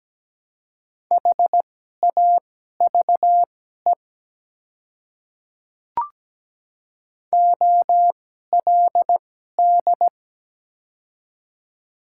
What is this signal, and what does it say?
Morse code, heard as a keyed tone, about 17 wpm, spaced as for 12 wpm, 690 Hz. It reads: HAVE OLD
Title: 2x Character Spacing - Top 300 Words - 17wpm